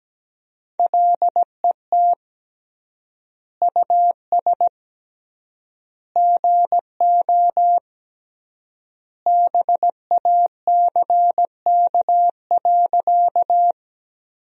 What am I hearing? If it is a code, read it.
Morse code, heard as a keyed tone, about 17 wpm, 700 Hz. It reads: LET US GO BACK.
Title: 3x Word-Spacing - Sentences from Top 300 Words - 17wpm